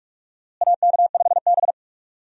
Morse code, heard as a keyed tone, about 46 wpm, 700 Hz. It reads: AK5B
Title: USA Call Signs - 4 Characters - Repeated 2x - 45wpm